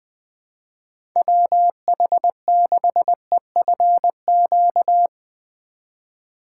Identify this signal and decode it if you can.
Morse code, heard as a keyed tone, about 20 wpm, 700 Hz. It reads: WH6EFQ
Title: USA Call Signs - 6 Characters - 4x Character Spacing - 20wpm